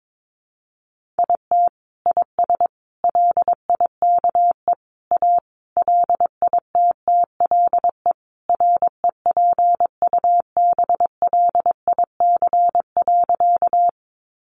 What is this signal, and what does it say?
Morse code, heard as a keyed tone, about 22 wpm, 700 Hz. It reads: IT IS LIKE A LITTLE REPUBLIC.
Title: Sentences from Top 1500 Words 22wpm